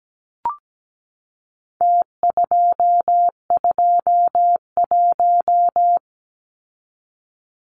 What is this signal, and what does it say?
Morse code, heard as a keyed tone, about 17 wpm, 700 Hz. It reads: T221
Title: Numbers - Contest Style - 4-Digit 17wpm